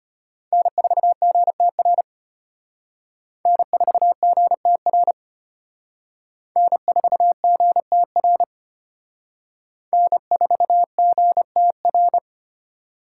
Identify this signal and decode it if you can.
Morse code, heard as a keyed tone, about 31 wpm, 700 Hz. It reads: N4GTR N4GTR N4GTR N4GTR
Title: Speed Racing - USA 1x3 Call Signs 25wpm